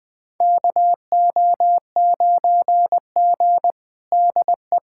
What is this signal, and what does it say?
Morse code, heard as a keyed tone, about 21 wpm, 700 Hz. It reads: KO9G DE
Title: USA Callsign DE Callsign - 4 Characters - 20wpm